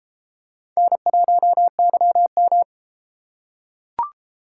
Morse code, heard as a keyed tone, about 31 wpm, 700 Hz. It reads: N1YM